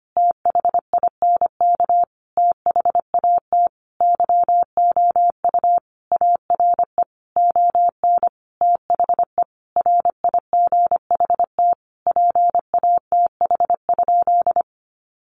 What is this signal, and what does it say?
Morse code, heard as a keyed tone, about 25 wpm, 700 Hz. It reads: THINK THAT YOU ARE ON THE RIGHT PATH?
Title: Sentences from Top 1400 Words 25wpm